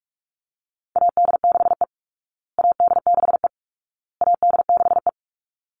Morse code, heard as a keyed tone, about 44 wpm, 700 Hz. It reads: AD6E AD6E AD6E
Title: Call Signs Repeated 2x - 45wpm